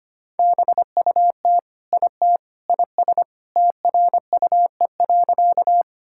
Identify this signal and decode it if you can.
Morse code, heard as a keyed tone, about 25 wpm, 700 Hz. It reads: BUT IT IS TRUE.
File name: practice / 25wpm / Sentences with 3-Word Phrases - 25wpm